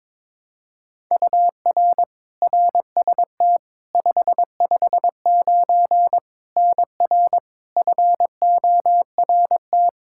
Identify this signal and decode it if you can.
Morse code, heard as a keyed tone, about 22 wpm, 700 Hz. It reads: UR RST 559 NR FORT
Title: Practice QSO - Exchange 1 - Beginning - 22wpm